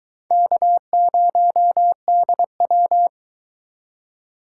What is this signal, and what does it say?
Morse code, heard as a keyed tone, about 22 wpm, 700 Hz. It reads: K0DW